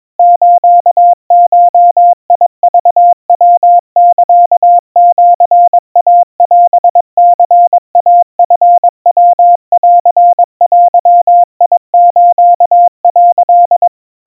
Morse code (Japanese Wabun code), heard as a keyed tone, about 22 wpm, 700 Hz. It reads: スゴクヤサシイオニイチヤンデス」
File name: 80 cpm 700 Hz Crispy